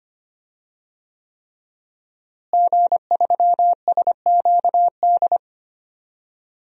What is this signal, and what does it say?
Morse code, heard as a keyed tone, about 25 wpm, 700 Hz. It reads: G3SQD
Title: DXCC-100 Call Signs - 5 Characters - Repeated 3x - 25wpm